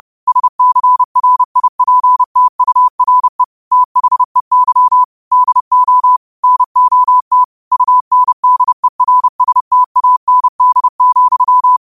Morse code, heard as a keyed tone, about 30 wpm, 1000 Hz. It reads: SCRIPTURE THEY DO NOT UNDERSTAND,
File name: qotdfiles / 30wpm